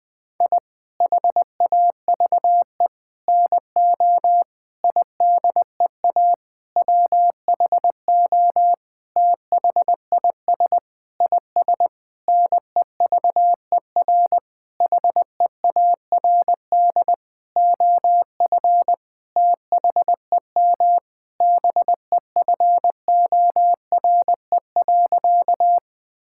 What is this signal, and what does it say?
Morse code, heard as a keyed tone, about 20 wpm, 700 Hz. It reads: I HAVE NO IDEA WHO THIS IS NEVER HEARD OF THEM BEFORE.
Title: Sentences with 9-Word Phrases - Spoken Beginning - 20wpm